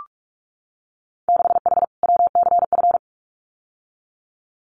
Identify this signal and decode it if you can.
Morse code, heard as a keyed tone, about 46 wpm, 700 Hz. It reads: 6H WCF
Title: Winter Field Day Exchanges Encore 45wpm